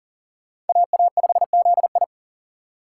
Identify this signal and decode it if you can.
Morse code, heard as a keyed tone, about 52 wpm, 700 Hz. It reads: AA5ZI